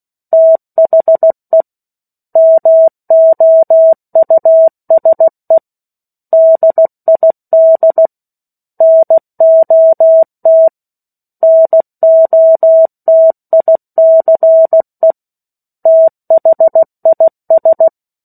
Morse code, heard as a keyed tone, about 16 wpm, 650 Hz. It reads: THE MOUSE DID NOT NOTICE THIS
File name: 03-Chapter-03 16 WPM